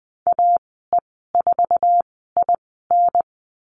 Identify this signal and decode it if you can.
Morse code, heard as a keyed tone, about 20 wpm, spaced as for 14 wpm, 700 Hz. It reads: AE4IN
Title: 062 Lesson 11g - Letter h - Call Signs 20wpm